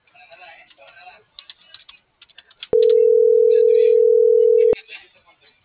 Ringback tone (North America), 440 + 480 Hz.